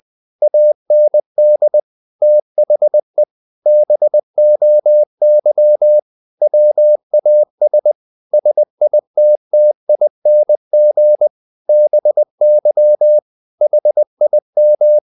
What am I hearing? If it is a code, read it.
Morse code, heard as a keyed tone, about 20 wpm, 600 Hz. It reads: AND THE BOY WAS SITTING BY HIM
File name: the old man and the sea part 2 20wpm